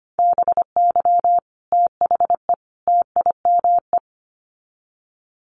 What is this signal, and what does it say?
Morse code, heard as a keyed tone, about 25 wpm, 700 Hz. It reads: BY THE TIME
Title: Sets of 3 Words 25wpm